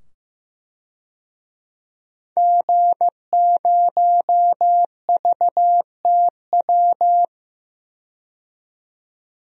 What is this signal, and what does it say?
Morse code, heard as a keyed tone, about 15 wpm, 700 Hz. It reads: G0VTW
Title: DXCC-100 Call Signs - 5 Characters - 15wpm